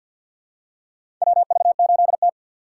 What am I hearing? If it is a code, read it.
Morse code, heard as a keyed tone, about 52 wpm, 700 Hz. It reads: WV8T
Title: USA Call Signs - 4 Characters - 2x Character Spacing - 50wpm